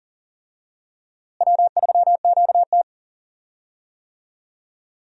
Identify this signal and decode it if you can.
Morse code, heard as a keyed tone, about 37 wpm, 700 Hz. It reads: W3QT
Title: USA Call Signs - 4 Characters - Repeated 2x - 40wpm